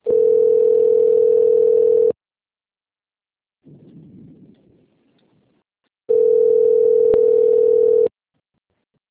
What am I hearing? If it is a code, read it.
Ringback tone (North America), 440 + 480 Hz.